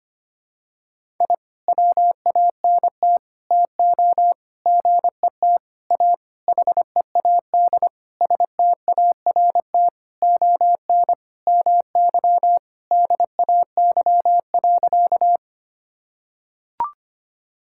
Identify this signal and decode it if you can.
Morse code, heard as a keyed tone, about 25 wpm, 700 Hz. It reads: I WANT TO GET A HEAD START ON MY DAY.
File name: practice / 25wpm / Sentences with 9-Word Phrases - 25wpm